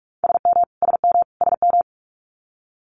Morse code, heard as a keyed tone, about 46 wpm, 700 Hz. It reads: SM SM SM
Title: Multi-letter Phonemes Repeated 3x 45wpm